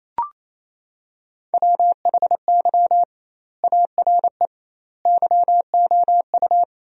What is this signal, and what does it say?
Morse code, heard as a keyed tone, about 28 wpm, 700 Hz. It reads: WHY ARE YOU
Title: 2x Word-Spacing - Sentences from Top 200 Words - 28wpm